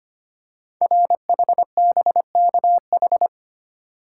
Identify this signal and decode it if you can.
Morse code, heard as a keyed tone, about 25 wpm, 700 Hz. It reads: RHBKH
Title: Code Groups - 5 Letters - 25wpm